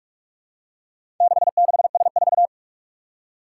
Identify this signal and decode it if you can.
Morse code, heard as a keyed tone, about 44 wpm, 700 Hz. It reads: 66S4
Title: Four Challenging Letter-Number Combinations 45wpm